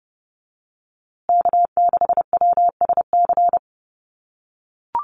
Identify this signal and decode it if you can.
Morse code, heard as a keyed tone, about 30 wpm, 700 Hz. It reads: K6WSC
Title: USA Call Signs Repeated 2x 30wpm